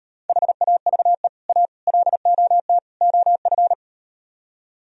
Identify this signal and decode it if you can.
Morse code, heard as a keyed tone, about 38 wpm, 700 Hz. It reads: HAVE A LOT OF